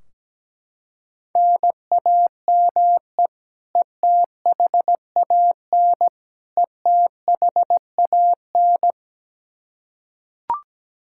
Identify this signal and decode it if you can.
Morse code, heard as a keyed tone, about 17 wpm, 700 Hz. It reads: NAME ETHAN ETHAN
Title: Practice QSO - Exchange 1 - Beginning - 17wpm